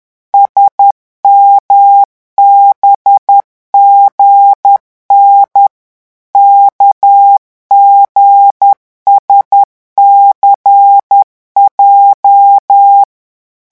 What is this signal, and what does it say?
Morse code, heard as a keyed tone, about 10 wpm, 800 Hz. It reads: SMBGN KGSCJ
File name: morse mp3 / revision8-vitesse 10 mots